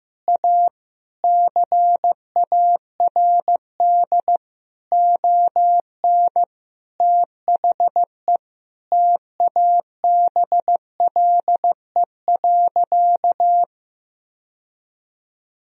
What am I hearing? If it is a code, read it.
Morse code, heard as a keyed tone, about 15 wpm, 700 Hz. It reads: A CARD ON THE TABLE.
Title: Sentences from Top 1000 Words - Review Part 4 - 15wpm